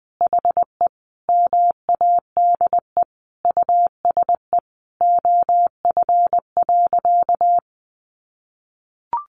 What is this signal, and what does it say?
Morse code, heard as a keyed tone, about 20 wpm, 700 Hz. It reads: HE MADE USE OF.
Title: Sentences from Top 1200 Words 20wpm